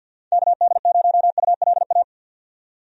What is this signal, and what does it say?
Morse code, heard as a keyed tone, about 51 wpm, 700 Hz. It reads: KD0ULA
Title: USA Call Signs - 6 Characters - 50wpm